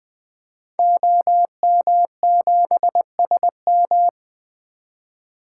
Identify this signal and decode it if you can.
Morse code, heard as a keyed tone, about 20 wpm, 700 Hz. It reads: OM7SM
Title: 2023 SOTA Activator Call Signs - 20wpm